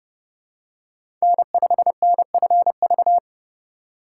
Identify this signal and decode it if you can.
Morse code, heard as a keyed tone, about 30 wpm, 700 Hz. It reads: N5NFV
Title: USA Call Signs - 5 Characters - 3x Character Spacing - 30wpm